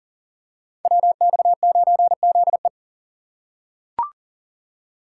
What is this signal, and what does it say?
Morse code, heard as a keyed tone, about 38 wpm, 700 Hz. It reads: WX9ZE
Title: Code Groups - 5 Letter:Numbers - 40wpm